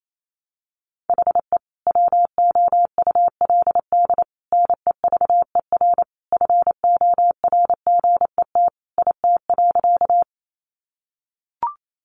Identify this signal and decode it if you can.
Morse code, heard as a keyed tone, about 28 wpm, 700 Hz. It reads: HE WOULD NEVER FORGET IT.
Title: Sentences from Top 1200 Words 28wpm